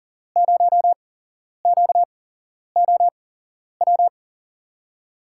Morse code, heard as a keyed tone, about 38 wpm, spaced as for 11 wpm, 700 Hz. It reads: 0QOW
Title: USA Call Signs - 5 Characters - 8x Character Spacing - 40wpm